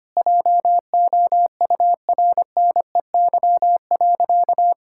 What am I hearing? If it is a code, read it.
Morse code, heard as a keyed tone, about 25 wpm, 700 Hz. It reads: JOURNEY.
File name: practice / 25wpm / Sentences with 2-Word Phrases - 25wpm